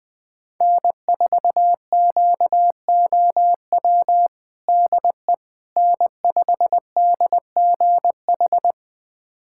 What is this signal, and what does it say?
Morse code, heard as a keyed tone, about 20 wpm, 700 Hz. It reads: N4QOW DE N5DGH
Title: USA Callsign DE Callsign - 5 Characters - 20wpm